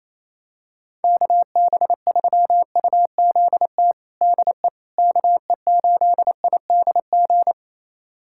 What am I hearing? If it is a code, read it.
Morse code, heard as a keyed tone, about 28 wpm, 700 Hz. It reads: KB3UZT DE KE8IDG
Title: USA Callsign DE Callsign 28wpm